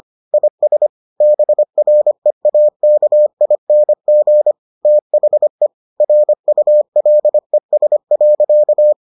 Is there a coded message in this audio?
Yes: Morse code from 0.3 to 8.9 s, heard as a keyed tone, about 25 wpm, 600 Hz. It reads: IS BREAKING THE RULES.